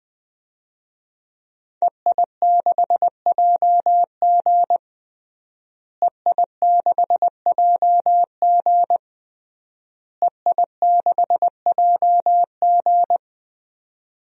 Morse code, heard as a keyed tone, about 20 wpm, 700 Hz. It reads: EI6JG EI6JG EI6JG
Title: DXCC-100 Call Signs - 5 Characters - Repeated 3x - 20wpm